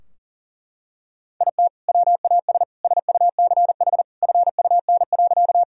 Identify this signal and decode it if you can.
Morse code, heard as a keyed tone, about 40 wpm, 700 Hz. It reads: IT WAS SUCH FUN.
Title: Sentences from Top 1000 Words - Review Part 4 - 40wpm